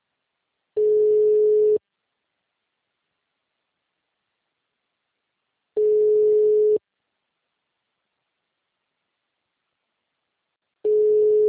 Ringback tone (Europe, 425 Hz).
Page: Sumare - SP